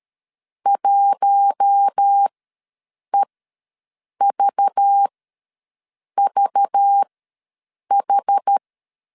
Morse code, heard as a keyed tone, about 13 wpm, spaced as for 7 wpm, 800 Hz. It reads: 1EVVH
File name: LEZIONI / 5 LBVC / 5 LBV ws 12wpm 3015